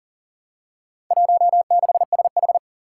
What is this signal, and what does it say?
Morse code, heard as a keyed tone, about 42 wpm, 700 Hz. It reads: 16SH
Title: Four Challenging Letter-Number Combinations 40wpm